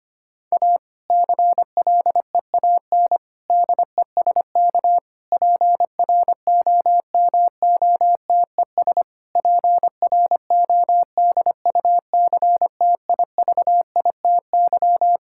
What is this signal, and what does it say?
Morse code, heard as a keyed tone, about 25 wpm, 700 Hz. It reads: A CLEAN DESK PROMOTES PRODUCTIVITY
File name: practice / 25wpm / Difficult Sentences with Hints - 25wpm